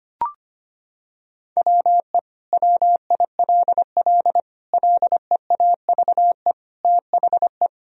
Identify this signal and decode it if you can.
Morse code, heard as a keyed tone, about 25 wpm, 700 Hz. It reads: WE WILL LEAVE THE
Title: Sentences from Top 1000 Words - Review Part 2 - 25wpm